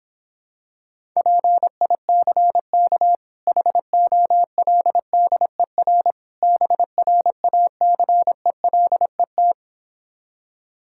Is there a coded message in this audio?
Yes: Morse code from 1.2 to 9.5 s, heard as a keyed tone, about 26 wpm, 700 Hz. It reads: PICK HOLDER BRACELET